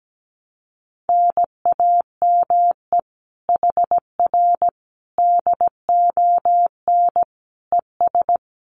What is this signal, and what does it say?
Morse code, heard as a keyed tone, about 17 wpm, 700 Hz. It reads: NAME HR DON ES
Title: Common QSO Phrases Encore 17wpm